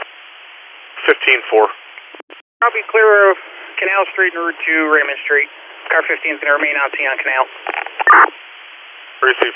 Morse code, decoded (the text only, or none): none